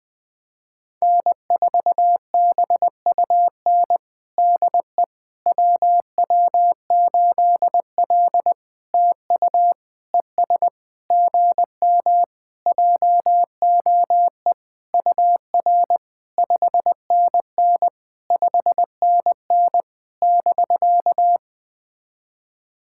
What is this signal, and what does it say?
Morse code, heard as a keyed tone, about 21 wpm, 700 Hz. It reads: N4BUN DE WW8L TU ES GM JOE UR 5NN 5NN <BK>
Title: POTA QSO Practice - 20wpm